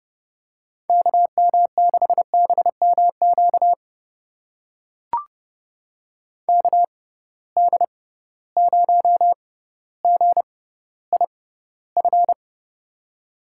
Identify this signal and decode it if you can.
Morse code, heard as a keyed tone, about 32 wpm, spaced as for 16 wpm, 700 Hz. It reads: KM6BMQ KD0GIF
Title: USA Call Signs - 6 Characters - 6x Character Spacing - 30wpm